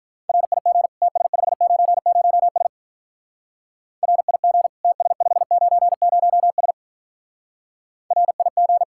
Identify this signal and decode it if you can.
Morse code, heard as a keyed tone, about 43 wpm, 700 Hz. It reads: RIG TS590S RIG TS590S RIG
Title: Speed-Racing Common QSO Phrases Encore - 35wpm